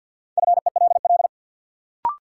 Morse code, heard as a keyed tone, about 46 wpm, 700 Hz. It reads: FELL
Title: Sentences from Top 1600 Words - New Word Drill - 50wpm